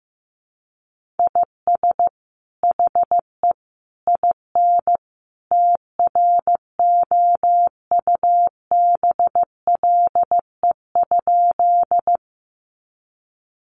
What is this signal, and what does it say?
Morse code, heard as a keyed tone, about 15 wpm, 700 Hz. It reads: IS HE IN TROUBLE?
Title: Sentences from Top 1200 Words 15wpm